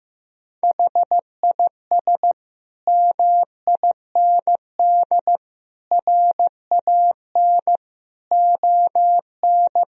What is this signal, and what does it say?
Morse code, heard as a keyed tone, about 15 wpm, 700 Hz. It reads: HIS MIND RAN ON
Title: Sentences from Top 1000 Words - Review Part 4 - 15wpm